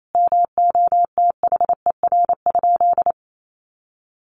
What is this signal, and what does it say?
Morse code, heard as a keyed tone, about 28 wpm, 700 Hz. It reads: MOTHER?
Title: Sentences from Top 200 Words 28wpm